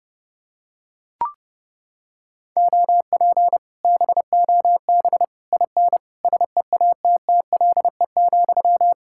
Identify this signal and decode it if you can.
Morse code, heard as a keyed tone, about 29 wpm, 700 Hz. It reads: OP BOB IN SEATTLE,